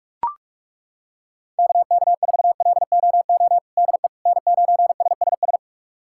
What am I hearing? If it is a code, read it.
Morse code, heard as a keyed tone, about 42 wpm, 700 Hz. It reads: KK4LOO DE N9SSS
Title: USA Callsign DE Callsign 45wpm